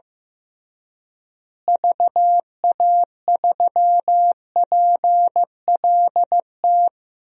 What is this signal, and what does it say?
Morse code, heard as a keyed tone, about 15 wpm, 700 Hz. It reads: VA3PLT
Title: DXCC-100 Call Signs - 6 Characters - Repeated 3x - 15wpm